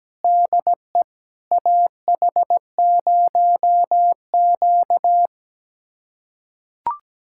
Morse code, heard as a keyed tone, about 17 wpm, 700 Hz. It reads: DE AH0Q
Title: USA Callsign DE Callsign - 4 Characters - 17wpm